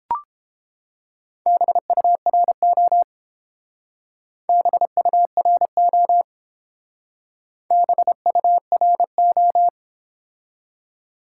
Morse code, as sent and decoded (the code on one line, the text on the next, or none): -... ..- .-. --- / -... ..- .-. --- / -... ..- .-. ---
BURO BURO BURO